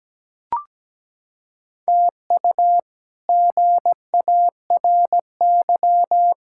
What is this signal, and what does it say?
Morse code, heard as a keyed tone, about 17 wpm, 700 Hz. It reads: TU GARY
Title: K1USN SST QSO Practice - 17wpm